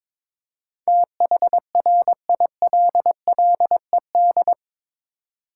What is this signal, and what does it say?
Morse code, heard as a keyed tone, about 21 wpm, 700 Hz. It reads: THRILLED
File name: practice / 22wpm / Eight Letter Words - 22wpm